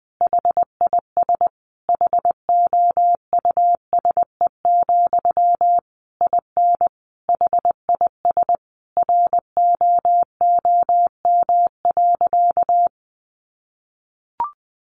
Morse code, heard as a keyed tone, about 20 wpm, 700 Hz. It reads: HIS HOUSE, IN HIS ROOM.
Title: Sentences from Top 300 Words 20wpm